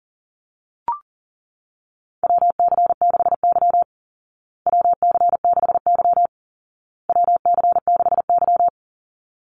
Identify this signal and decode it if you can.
Morse code, heard as a keyed tone, about 40 wpm, 700 Hz. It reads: WC6Y WC6Y WC6Y
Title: Call Signs Repeated 2x - 40wpm